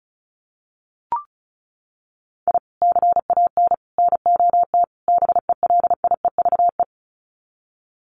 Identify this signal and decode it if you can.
Morse code, heard as a keyed tone, about 34 wpm, 700 Hz. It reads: I CAN NOT BELIEVE
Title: Sets of 4 Words 35wpm